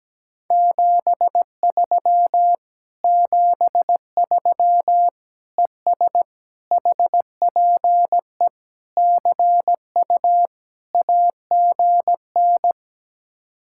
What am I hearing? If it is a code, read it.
Morse code, heard as a keyed tone, about 17 wpm, 700 Hz. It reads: 73 73 ES HPE CU AGN
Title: Speed-Racing Common QSO Phrases Encore - 17wpm